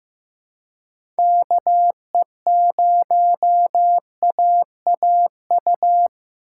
Morse code, heard as a keyed tone, about 15 wpm, 700 Hz. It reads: KE0AAU
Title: USA Call Signs - 6 Characters - 4x Character Spacing - 15wpm